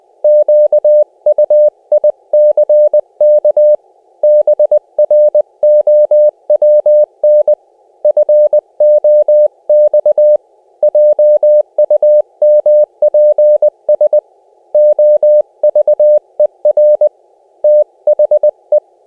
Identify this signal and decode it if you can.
Morse code, heard as a keyed tone, about 20 wpm, 600 Hz. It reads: QUICK BROWN FOX JUMPS OVER THE
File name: cw / cwfox / Op C 600 Hz